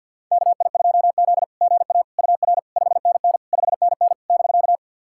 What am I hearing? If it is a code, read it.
Morse code, heard as a keyed tone, about 52 wpm, 690 Hz. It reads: KI2Z GA UR 5NN 5NN <BK>
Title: SOTA QSO Practice 50wpm